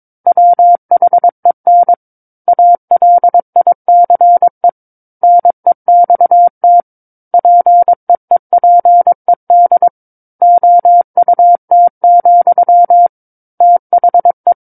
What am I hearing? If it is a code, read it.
Morse code, heard as a keyed tone, about 22 wpm, 700 Hz. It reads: WHEN ALICE NEXT PEEPED OUT, THE
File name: 06-Chapter-06 22 WPM